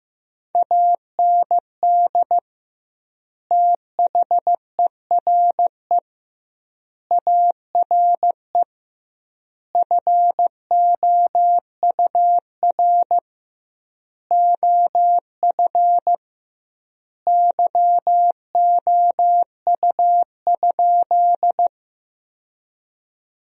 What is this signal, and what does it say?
Morse code, heard as a keyed tone, about 16 wpm, 700 Hz. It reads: AND THERE ARE FOUR OF YOU?
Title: 2x Word-Spacing - Sentences from Top 300 Words - 15wpm